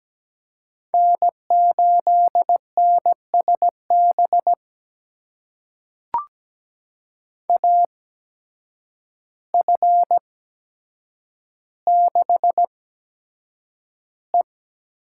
Morse code, heard as a keyed tone, about 17 wpm, spaced as for 8 wpm, 700 Hz. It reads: N8NSB AF6E
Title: USA Call Signs - 5 Characters - 8x Character Spacing - 17wpm